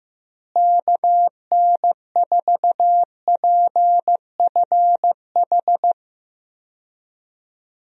Morse code, heard as a keyed tone, about 15 wpm, 700 Hz. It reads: KN4PFH
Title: USA Call Signs - 6 Characters - Repeated 3x - 15wpm